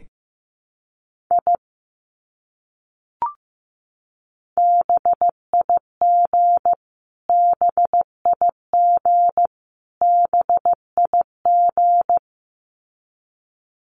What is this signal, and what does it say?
Morse code, heard as a keyed tone, about 15 wpm, 700 Hz. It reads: I BIG BIG BIG
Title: Top 300 Words Repeated 15wpm